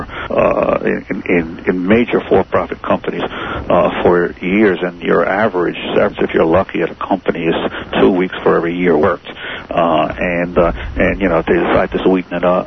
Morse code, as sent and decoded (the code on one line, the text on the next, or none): none